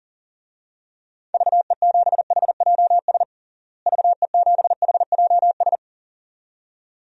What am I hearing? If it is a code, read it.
Morse code, heard as a keyed tone, about 37 wpm, 700 Hz. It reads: VE7HJS VE7HJS
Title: DXCC-100 Call Signs - 6 Characters - Repeated 2x - 40wpm